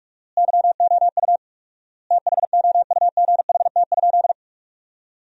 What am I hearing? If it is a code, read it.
Morse code, heard as a keyed tone, about 47 wpm, 700 Hz. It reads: YOU THOUGHT?
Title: 4x Word-Spacing - Sentences from Top 300 Words - 45wpm